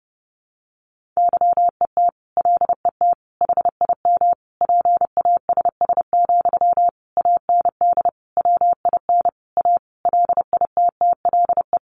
Morse code, heard as a keyed tone, about 30 wpm, 700 Hz. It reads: YET LET HIM PASS, AND WIN A LITTLE
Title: Sentences from Top 1500 Words 30wpm